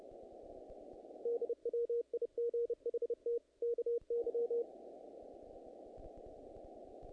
Morse code, heard as a keyed tone, about 30 wpm, 470 Hz. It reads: DWIGHT KY
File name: contests / naqp-cw-k5tr-op-k5pi- -1900